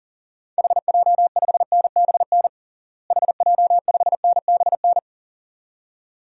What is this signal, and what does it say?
Morse code, heard as a keyed tone, about 40 wpm, 700 Hz. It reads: HJ5NBN HJ5NBN
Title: DXCC-100 Call Signs - 6 Characters - Repeated 2x - 40wpm